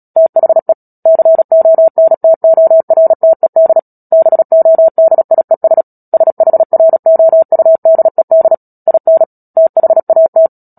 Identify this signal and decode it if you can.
Morse code, heard as a keyed tone, about 37 wpm, 650 Hz. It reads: THE CONTORTED BODIES SHROUDED IN THAT
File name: Book Two / 10-Chapter 10 - Epilogue - 36 WPM